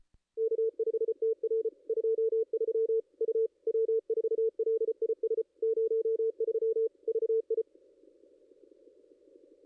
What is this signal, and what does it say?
Morse code, heard as a keyed tone, about 33 wpm, 450 Hz. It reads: K5TR 23 U W4LIS 03 VI
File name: contests / ss-cw-k5tr-op-n5rz -2200